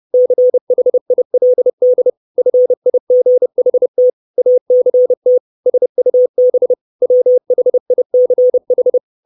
Morse code, heard as a keyed tone, about 30 wpm, 500 Hz. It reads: CHILD FIGHT ACT SUB WHICH
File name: 632 Words 30WPM